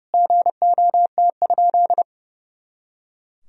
Morse code, heard as a keyed tone, about 32 wpm, 700 Hz. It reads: GOT?
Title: Sentences from Top 300 Words and Abbreviations 30wpm